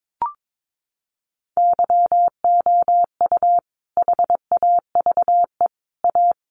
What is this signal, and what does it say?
Morse code, heard as a keyed tone, about 22 wpm, 700 Hz. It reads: YOU HAVE A
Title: Sentences from Top 1100 Words 22wpm